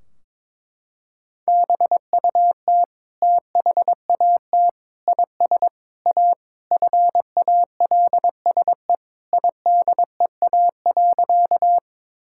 Morse code, heard as a keyed tone, about 22 wpm, 700 Hz. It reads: BUT THAT IS A FALSE IDEA.
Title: Sentences from Top 1600 Words 22wpm